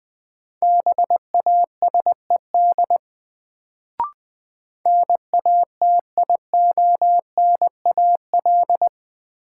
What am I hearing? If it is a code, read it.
Morse code, heard as a keyed tone, about 20 wpm, 700 Hz. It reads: BASED NATIONAL
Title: Sets of 2 Words - New Word Drill - 20wpm